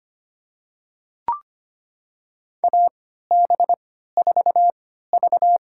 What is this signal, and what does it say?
Morse code, heard as a keyed tone, about 25 wpm, spaced as for 14 wpm, 700 Hz. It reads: AB4V
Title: USA Call Signs - 4 Characters - 3x Character Spacing - 25wpm